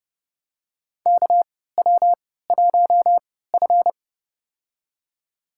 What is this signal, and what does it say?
Morse code, heard as a keyed tone, about 30 wpm, spaced as for 17 wpm, 700 Hz. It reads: KW1F
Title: USA Call Signs - 4 Characters - 3x Character Spacing - 30wpm